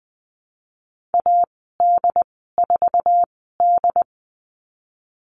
Morse code, heard as a keyed tone, about 20 wpm, spaced as for 15 wpm, 700 Hz. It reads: AD4D
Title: 161 Lesson 26g - Letter g - Call Signs 20wpm